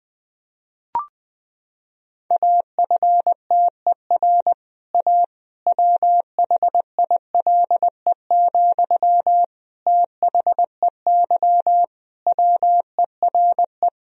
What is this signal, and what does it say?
Morse code, heard as a keyed tone, about 20 wpm, 700 Hz. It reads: AFTER A WHILE, THEY WERE
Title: Sentences from Top 1000 Words - Review Part 1 - 20wpm